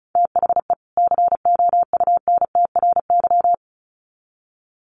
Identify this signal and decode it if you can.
Morse code, heard as a keyed tone, about 34 wpm, 700 Hz. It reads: THE COUNTRY
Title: Sets of 2 Words 35wpm